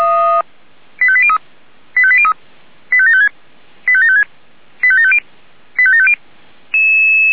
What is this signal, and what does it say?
Five-tone selective call (ZVEI), heard as quick runs of five tones, tones of about 70 ms. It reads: 85803 85803 85756 85756 85750 85750
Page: SDR/ZVEI